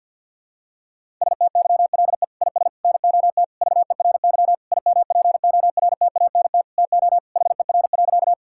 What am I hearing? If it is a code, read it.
Morse code, heard as a keyed tone, about 49 wpm, 700 Hz. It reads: STYLE IS NOT VERY IMPORTANT TO HER.
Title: Sentences from Top 1000 Words - Review Part 5 - 50wpm